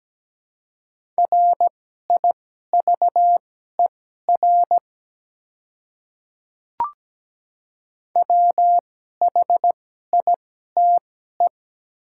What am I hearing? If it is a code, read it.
Morse code, heard as a keyed tone, about 17 wpm, spaced as for 12 wpm, 700 Hz. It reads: RIVER WHITE
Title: 2x Character Spacing - Top 300 Words - 17wpm